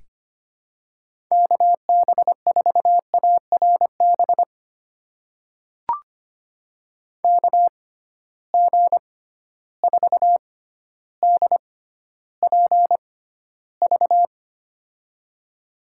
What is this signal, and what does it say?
Morse code, heard as a keyed tone, about 25 wpm, spaced as for 13 wpm, 700 Hz. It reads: KB4ARB KG4DPV
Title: USA Call Signs - 6 Characters - 6x Character Spacing - 25wpm